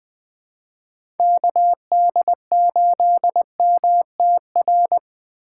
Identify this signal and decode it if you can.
Morse code, heard as a keyed tone, about 20 wpm, 700 Hz. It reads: KD8MTR
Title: USA Call Signs - 6 Characters - 2x Character Spacing - 20wpm